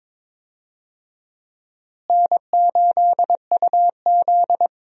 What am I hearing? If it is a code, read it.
Morse code, heard as a keyed tone, about 21 wpm, 700 Hz. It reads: N8UZ
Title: USA Call Signs - 4 Characters - Repeated 3x - 22wpm